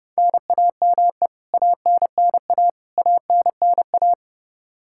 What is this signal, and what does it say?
Morse code, heard as a keyed tone, about 32 wpm, 700 Hz. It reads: NAME ANNA ANNA
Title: Practice QSO - Exchange 1 - Beginning - 30wpm